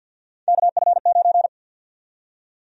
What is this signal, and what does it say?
Morse code, heard as a keyed tone, about 50 wpm, 700 Hz. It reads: KF9
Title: USA Call Sign Prefixes - 50wpm